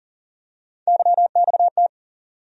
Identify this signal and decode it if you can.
Morse code, heard as a keyed tone, about 40 wpm, 700 Hz. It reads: YXT
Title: USA Call Sign Postfixes - 3 Letters 40wpm